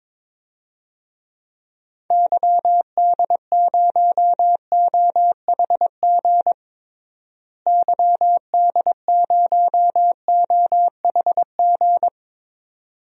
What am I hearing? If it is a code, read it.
Morse code, heard as a keyed tone, about 22 wpm, 700 Hz. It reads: YD0OHG YD0OHG
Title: DXCC-100 Call Signs - 6 Characters - Repeated 2x - 22wpm